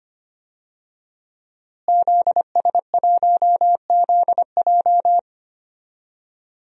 Morse code, heard as a keyed tone, about 25 wpm, 700 Hz. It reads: ZS1ZJ
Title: DXCC-100 Call Signs - 5 Characters - Repeated 3x - 25wpm